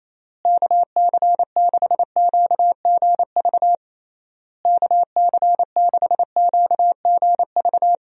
Morse code, heard as a keyed tone, about 28 wpm, 700 Hz. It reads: KC6QGV KC6QGV
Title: USA Call Signs - 6 Characters - Repeated 3x - 28wpm